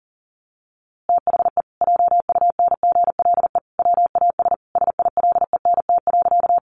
Morse code, heard as a keyed tone, about 40 wpm, 700 Hz. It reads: THE JUNGLE WAS SILENT.
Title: A Sound Of Thunder 40wpm